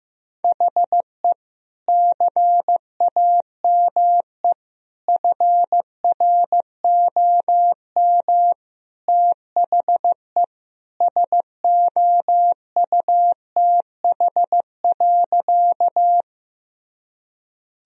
Morse code, heard as a keyed tone, about 15 wpm, 700 Hz. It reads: HE CAME FROM THE SOUTH.
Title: Sentences from Top 1000 Words - Review Part 2 - 15wpm